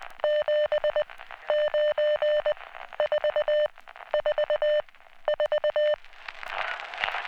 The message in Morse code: 7 9 4 4 4